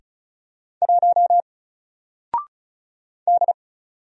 Morse code, heard as a keyed tone, about 36 wpm, 700 Hz. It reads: 1 D
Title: Single Letter-Number Alt 35wpm